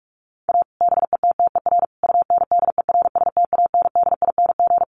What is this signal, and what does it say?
Morse code, heard as a keyed tone, about 44 wpm, 700 Hz. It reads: A BETTER UNDERSTANDING
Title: Sets of 5 Words Encore2 45wpm